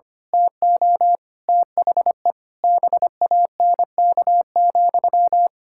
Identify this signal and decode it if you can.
Morse code, heard as a keyed tone, about 25 wpm, 700 Hz. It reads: TO THE BANK,
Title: Pangram Challenge v2 - 25wpm